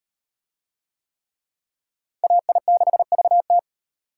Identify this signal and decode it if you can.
Morse code, heard as a keyed tone, about 38 wpm, 700 Hz. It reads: AI6VT